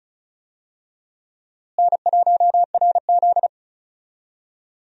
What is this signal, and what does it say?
Morse code, heard as a keyed tone, about 34 wpm, 700 Hz. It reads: N1RZ
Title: USA Call Signs - 4 Characters - Repeated 3x - 35wpm